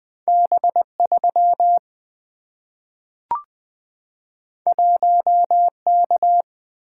Morse code, heard as a keyed tone, about 20 wpm, 700 Hz. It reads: B3 1K
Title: Single-Letter Single-Number 20wpm